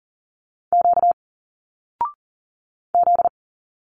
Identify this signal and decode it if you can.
Morse code, heard as a keyed tone, about 40 wpm, 700 Hz. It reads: Q Z
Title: Single Letters 40wpm